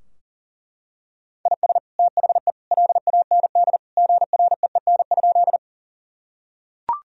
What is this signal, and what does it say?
Morse code, heard as a keyed tone, about 40 wpm, 700 Hz. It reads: IS THE LAND GREEN?
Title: Sentences from Top 1000 Words - Review Part 2 - 40wpm